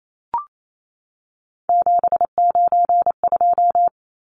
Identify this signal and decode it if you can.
Morse code, heard as a keyed tone, about 28 wpm, 700 Hz. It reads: 792
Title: Numbers - 3-Digit 28wpm